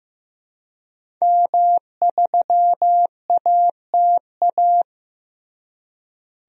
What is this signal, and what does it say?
Morse code, heard as a keyed tone, about 15 wpm, 700 Hz. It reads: M3ATA
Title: DXCC-100 Call Signs - 5 Characters - Repeated 3x - 15wpm